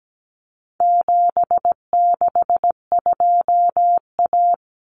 Morse code, heard as a keyed tone, about 17 wpm, 700 Hz. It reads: 762A
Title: Common Cut Numbers - 4-Digit 17wpm